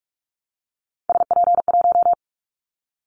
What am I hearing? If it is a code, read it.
Morse code, heard as a keyed tone, about 44 wpm, 700 Hz. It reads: SP1
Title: Callsign Prefixes for Top 25 DXCC - Repeated 3x - 45wpm